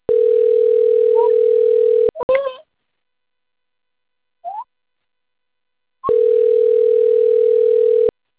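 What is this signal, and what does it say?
Ringback tone (North America), 440 + 480 Hz.
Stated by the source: Outbound telephone call recording from Cornelio Procopio.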